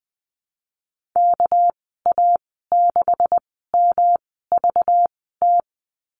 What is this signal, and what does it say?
Morse code, heard as a keyed tone, about 20 wpm, spaced as for 14 wpm, 700 Hz. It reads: KA6MVT
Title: 207 Lesson 33g - Letter j - Call Signs 20wpm